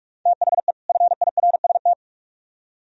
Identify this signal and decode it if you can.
Morse code, heard as a keyed tone, about 44 wpm, 700 Hz. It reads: THE FIRST